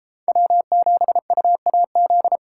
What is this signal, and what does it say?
Morse code, heard as a keyed tone, about 32 wpm, 700 Hz. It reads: W7UAZ